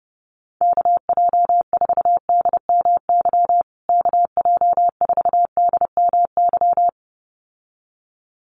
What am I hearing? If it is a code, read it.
Morse code, heard as a keyed tone, about 30 wpm, 700 Hz. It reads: KJ4DMY KJ4DMY
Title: USA Call Signs Repeated 2x 30wpm